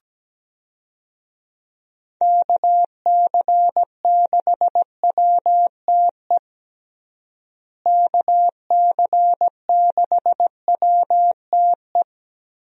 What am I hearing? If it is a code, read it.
Morse code, heard as a keyed tone, about 17 wpm, 700 Hz. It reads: KC6WTE KC6WTE
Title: USA Call Signs - 6 Characters - Repeated 2x - 17wpm